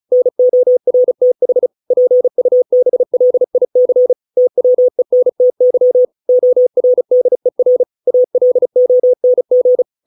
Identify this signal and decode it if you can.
Morse code, heard as a keyed tone, about 35 wpm, 500 Hz. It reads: NORTH PUBLIC TWENTY ORDER ALONG